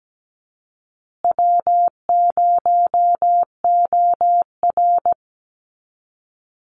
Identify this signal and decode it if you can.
Morse code, heard as a keyed tone, about 17 wpm, 700 Hz. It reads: W0OR
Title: Easy Call Signs Repeated 3x - 17wpm